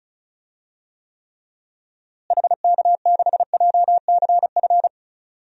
Morse code, heard as a keyed tone, about 36 wpm, 700 Hz. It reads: HK6JCF
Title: DXCC-100 Call Signs - 6 Characters - Repeated 2x - 35wpm